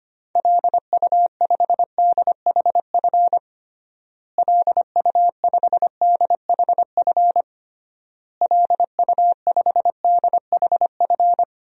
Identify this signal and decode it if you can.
Morse code, heard as a keyed tone, about 25 wpm, 700 Hz. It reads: LU5DHF LU5DHF LU5DHF
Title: DXCC-100 Call Signs - 6 Characters - Repeated 3x - 25wpm